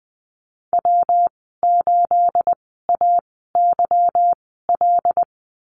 Morse code, heard as a keyed tone, about 20 wpm, spaced as for 14 wpm, 700 Hz. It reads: W8AYL
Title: 245 Lesson 39g - Letter z - Call Signs 20wpm